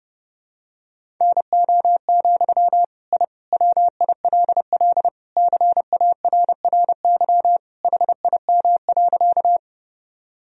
Morse code, heard as a keyed tone, about 30 wpm, 700 Hz. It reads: NO, I WILL CARRY HIM.
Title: Sentences from Top 1000 Words - Review Part 1 - 30wpm